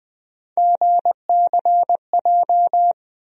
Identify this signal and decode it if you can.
Morse code, heard as a keyed tone, about 20 wpm, 700 Hz. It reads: GCJ
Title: USA Call Sign Postfixes - 3 Letters 20wpm